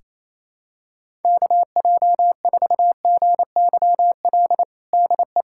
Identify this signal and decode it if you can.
Morse code, heard as a keyed tone, about 28 wpm, 700 Hz. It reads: KJ4GYL DE
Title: USA Callsign DE Callsign - 6 Characters - 28wpm